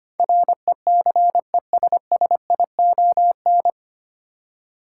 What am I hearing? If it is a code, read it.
Morse code, heard as a keyed tone, about 25 wpm, 700 Hz. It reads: RECESSION
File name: Nine Letter Words - 25wpm